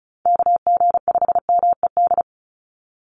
Morse code, heard as a keyed tone, about 34 wpm, 700 Hz. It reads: KG5MED
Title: USA Call Signs 35wpm